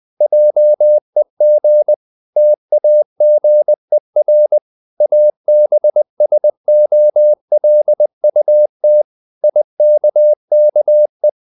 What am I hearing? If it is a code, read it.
Morse code, heard as a keyed tone, about 20 wpm, 600 Hz. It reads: JEG TAGER ABSOLUT IKKE
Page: Ferie 20 wpm